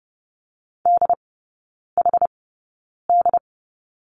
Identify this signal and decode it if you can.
Morse code, heard as a keyed tone, about 30 wpm, spaced as for 9 wpm, 700 Hz. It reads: DHD
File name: course / 30:10wpm / 087 Lesson 15e - Number 5 - Three Characters 30wpm